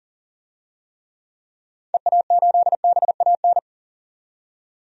Morse code, heard as a keyed tone, about 40 wpm, 700 Hz. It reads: EA8BAN
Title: DXCC-100 Call Signs - 6 Characters - 40wpm